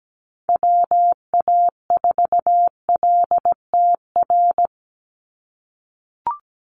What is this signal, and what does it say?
Morse code, heard as a keyed tone, about 17 wpm, 700 Hz. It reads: WA4LTR
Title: USA Call Signs 17wpm